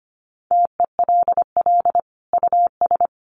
Morse code, heard as a keyed tone, about 25 wpm, 700 Hz. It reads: TELL US
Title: Sentences from Top 1200 Words 25wpm